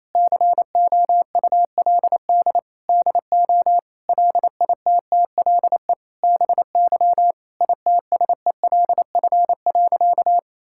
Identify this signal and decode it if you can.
Morse code, heard as a keyed tone, about 28 wpm, 700 Hz. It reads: COULD DO LITTLE BY ITSELF.
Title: Sentences from Top 1000 Words - Review Part 5 - 28wpm